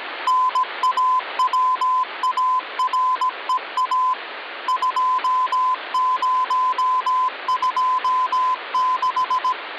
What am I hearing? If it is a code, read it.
Morse code (Russian alphabet), heard as a keyed tone, about 17 wpm, 1020 Hz, with noise. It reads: НАВАРЕА 2026
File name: original